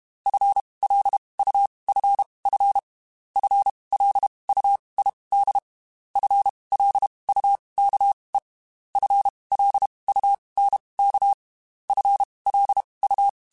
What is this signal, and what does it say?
Morse code, heard as a keyed tone, about 32 wpm, spaced as for 23 wpm, 800 Hz. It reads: FLUFF FLUID FLUKE FLUNK FLU
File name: Spaced / FL Runs sp mp3s / FL Runs sp @30